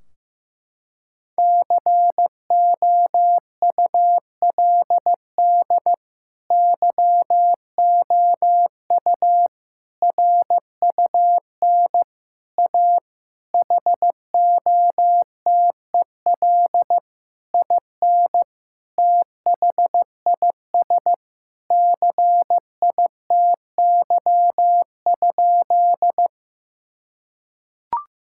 Morse code, heard as a keyed tone, about 15 wpm, 700 Hz. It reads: COULD YOU RUN A HOTEL IN THIS CITY?
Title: Sentences from Top 1600 Words 15wpm